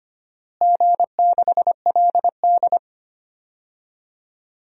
Morse code, heard as a keyed tone, about 25 wpm, 700 Hz. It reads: G6LD
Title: DXCC-100 Call Signs - 4 Characters - Repeated 3x - 25wpm